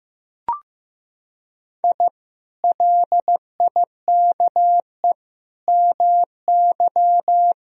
Morse code, heard as a keyed tone, about 15 wpm, 700 Hz. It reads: I LIKE MY
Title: Sentences from Top 1000 Words - Review Part 3 - 15wpm